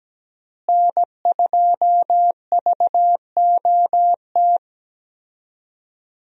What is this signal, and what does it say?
Morse code, heard as a keyed tone, about 17 wpm, 700 Hz. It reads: N2VOT
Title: USA Call Signs - 5 Characters - Repeated 3x - 17wpm